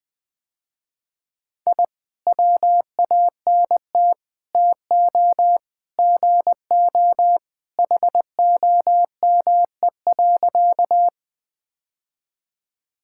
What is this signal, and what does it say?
Morse code, heard as a keyed tone, about 21 wpm, 700 Hz. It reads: I WANT TO GO HOME.
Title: Sentences with 3-Word Phrases - Spoken Beginning - 20wpm